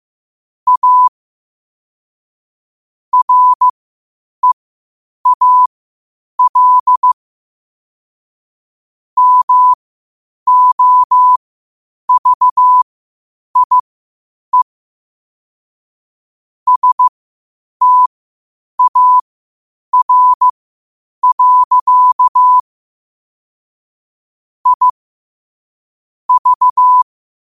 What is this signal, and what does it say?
Morse code, heard as a keyed tone, about 15 wpm, spaced as for 8 wpm, 1000 Hz. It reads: A REAL MOVIE STAR. I V
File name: qotdfiles / 8wpm